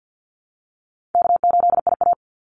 Morse code, heard as a keyed tone, about 50 wpm, 700 Hz. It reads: K8IA
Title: Call Signs Repeated 3x - 50wpm